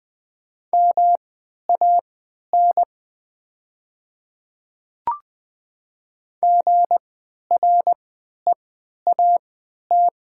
Morse code, heard as a keyed tone, about 20 wpm, spaced as for 11 wpm, 690 Hz. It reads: MAN GREAT
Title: 3x Character Spacing - Top 200 Words - 20wpm